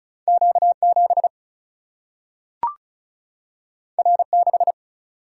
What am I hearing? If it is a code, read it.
Morse code, heard as a keyed tone, about 36 wpm, 700 Hz. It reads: Q 7 R 6
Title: Single-Letter Single-Number 35wpm